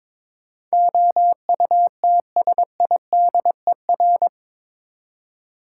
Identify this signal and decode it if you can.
Morse code, heard as a keyed tone, about 22 wpm, 700 Hz. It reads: OUTSIDER
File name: practice / 22wpm / Eight Letter Words - 22wpm